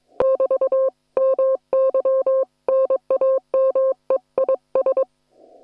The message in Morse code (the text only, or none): <BT> MY NAME IS